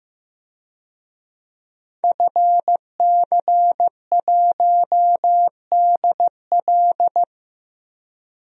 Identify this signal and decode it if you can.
Morse code, heard as a keyed tone, about 15 wpm, 700 Hz. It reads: FC1DL
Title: DXCC-100 Call Signs - 5 Characters - 15wpm